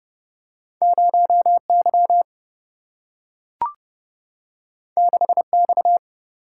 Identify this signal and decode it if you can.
Morse code, heard as a keyed tone, about 30 wpm, 700 Hz. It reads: 0 Y 6 X
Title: Single-Letter Single-Number 30wpm